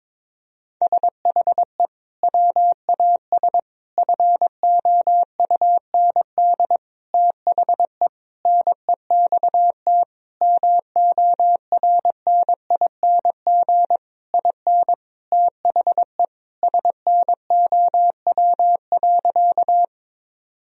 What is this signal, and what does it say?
Morse code, heard as a keyed tone, about 22 wpm, 700 Hz. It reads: SHE WAS FOUND THE NEXT MORNING IN THE SNOW.
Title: Sentences from Top 1000 Words - Review Part 3 - 22wpm